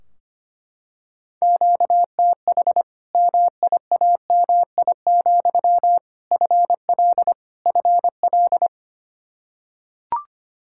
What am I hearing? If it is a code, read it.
Morse code, heard as a keyed tone, about 25 wpm, 700 Hz. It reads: QTH MIAMI, FL FL
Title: Practice QSO - Exchange 1 - Beginning - 25wpm